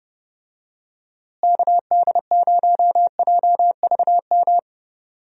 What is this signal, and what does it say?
Morse code, heard as a keyed tone, about 30 wpm, 700 Hz. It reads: KD0JVM
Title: USA Call Signs - 6 Characters - Repeated 2x - 30wpm